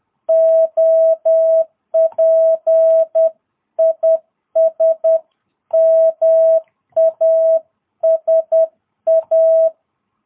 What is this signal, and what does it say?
Morse code, heard as a keyed tone, about 10 wpm, 660 Hz. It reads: OP IS MASA
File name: jh8cbh / 2022cw